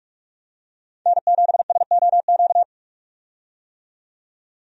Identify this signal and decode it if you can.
Morse code, heard as a keyed tone, about 41 wpm, 700 Hz. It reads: N7SOQ
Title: USA Call Signs - 5 Characters - 45wpm